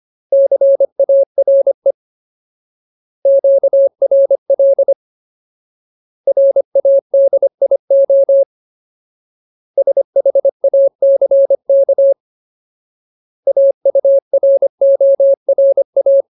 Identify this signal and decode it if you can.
Morse code, heard as a keyed tone, about 25 wpm, 550 Hz. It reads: CARE QRL RADIO SHACK AURORA
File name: Ultimate Word List / 25WPM Ultimate